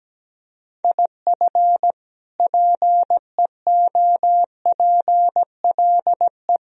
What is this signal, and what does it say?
Morse code, heard as a keyed tone, about 17 wpm, 700 Hz. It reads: IF PEOPLE
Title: Sentences from Top 1000 Words - Review Part 3 - 17wpm